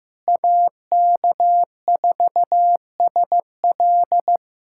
Morse code, heard as a keyed tone, about 16 wpm, 700 Hz. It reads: AK4SL
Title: USA Callsign DE Callsign - 5 Characters - 15wpm